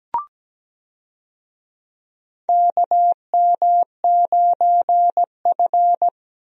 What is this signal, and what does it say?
Morse code, heard as a keyed tone, about 17 wpm, 700 Hz. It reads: KM9F
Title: USA Call Signs - 4 Characters - 17wpm